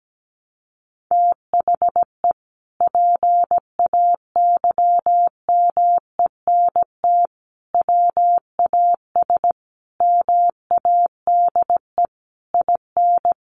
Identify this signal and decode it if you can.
Morse code, heard as a keyed tone, about 17 wpm, 700 Hz. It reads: THE PAYMENT WAS MADE IN
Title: Sentences from Top 1200 Words 17wpm